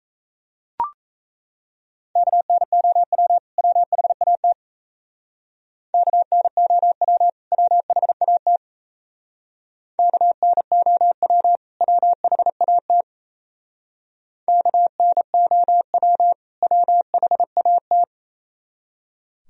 Morse code, heard as a keyed tone, about 37 wpm, 700 Hz. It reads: KNOW WHAT KNOW WHAT KNOW WHAT KNOW WHAT